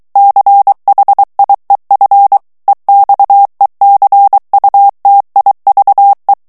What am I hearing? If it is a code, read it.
Morse code, heard as a keyed tone, about 23 wpm, 800 Hz. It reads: CHIEF EXECUTIVE
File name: ANGLAIS texte clair 3 vitesse 22 mots minute